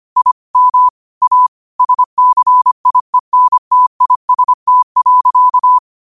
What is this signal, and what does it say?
Morse code, heard as a keyed tone, about 25 wpm, 1000 Hz. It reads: I M A SCIENTIST.
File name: qotdfiles / 25wpm